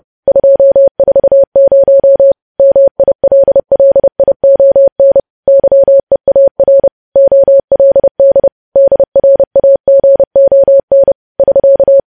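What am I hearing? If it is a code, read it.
Morse code, heard as a keyed tone, about 30 wpm, 550 Hz. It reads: 240 MILLION YEAR OLD DRAGON <SK>
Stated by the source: News headlines sent at 30 wpm.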